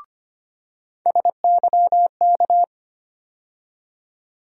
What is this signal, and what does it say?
Morse code, heard as a keyed tone, about 25 wpm, 700 Hz. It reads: SYK
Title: USA Call Sign Postfixes - 3 Letters 25wpm